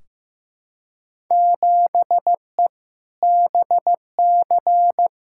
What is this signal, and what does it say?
Morse code, heard as a keyed tone, about 16 wpm, 700 Hz. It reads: 7E BC
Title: Field Day Exchanges Encore - 15wpm